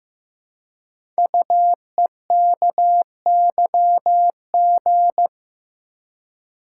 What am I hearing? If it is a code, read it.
Morse code, heard as a keyed tone, about 15 wpm, 700 Hz. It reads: UEKYG
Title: Code Groups - 5 Letters - 15wpm